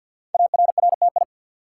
Morse code, heard as a keyed tone, about 48 wpm, 690 Hz. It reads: ARRTI
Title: Code Groups - 5 Letters - 50wpm